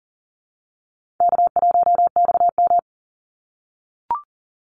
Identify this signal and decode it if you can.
Morse code, heard as a keyed tone, about 40 wpm, 700 Hz. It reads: K1XM
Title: Easy Call Signs - 40wpm